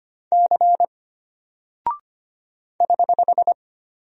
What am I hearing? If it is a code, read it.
Morse code, heard as a keyed tone, about 25 wpm, 700 Hz. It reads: C <HH>
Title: Single Character (Letter:Number:Punctuation:Prosign) Alt 25wpm